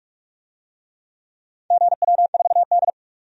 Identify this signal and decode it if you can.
Morse code, heard as a keyed tone, about 41 wpm, 700 Hz. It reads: GW4D
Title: DXCC-100 Call Signs - 4 Characters - 45wpm